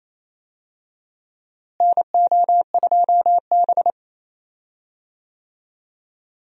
Morse code, heard as a keyed tone, about 28 wpm, 700 Hz. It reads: NO2B
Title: USA Call Signs - 4 Characters - Repeated 3x - 28wpm